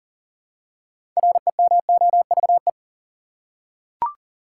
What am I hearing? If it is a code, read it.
Morse code, heard as a keyed tone, about 40 wpm, 700 Hz. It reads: REMOVE
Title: Six Letter Words - 40wpm